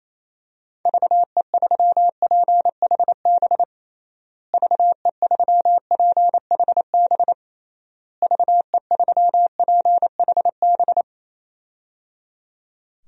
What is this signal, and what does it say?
Morse code, heard as a keyed tone, about 28 wpm, 700 Hz. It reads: VE3PHB VE3PHB VE3PHB
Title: DXCC-100 Call Signs - 6 Characters - Repeated 3x - 28wpm